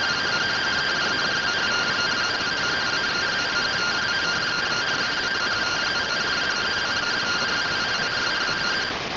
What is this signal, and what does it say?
Two-tone teleprinter signal, about 170 Hz shift.